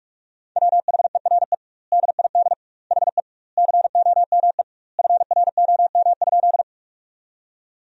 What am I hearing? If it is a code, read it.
Morse code, heard as a keyed tone, about 46 wpm, 700 Hz. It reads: WHERE DID HE COME FROM?
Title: 2x Word-Spacing - Sentences from Top 200 Words - 45wpm